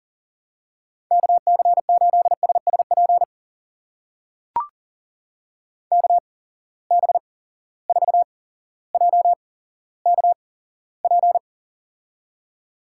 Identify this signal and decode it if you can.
Morse code, heard as a keyed tone, about 40 wpm, spaced as for 17 wpm, 700 Hz. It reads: KC8SSP KB4JKP
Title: USA Call Signs - 6 Characters - 8x Character Spacing - 40wpm